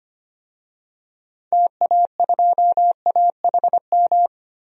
Morse code, heard as a keyed tone, about 25 wpm, 700 Hz. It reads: TA2AHM
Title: DXCC-100 Call Signs - 6 Characters - 25wpm